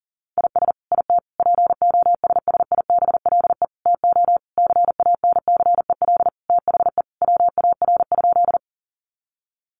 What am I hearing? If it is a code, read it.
Morse code, heard as a keyed tone, about 40 wpm, 700 Hz. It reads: IS IT POSSIBLE TO CANCEL THE WAR?
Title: Sentences from Top 1200 Words 40wpm